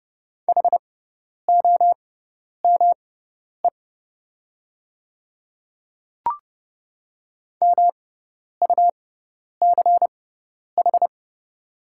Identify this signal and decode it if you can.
Morse code, heard as a keyed tone, about 30 wpm, spaced as for 10 wpm, 700 Hz. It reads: HOME MUCH
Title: 6x Character Spacing - Top 200 Words - 30wpm